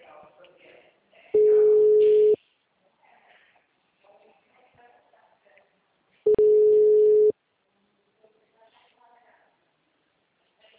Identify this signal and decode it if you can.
Ringback tone (Europe, 425 Hz), with speech.
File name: out- -2004- - - .2744